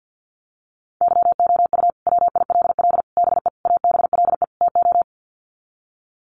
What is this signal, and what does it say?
Morse code, heard as a keyed tone, about 50 wpm, 700 Hz. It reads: YOU WILL BE ABLE TO
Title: Sets of 5 Words Encore2 50wpm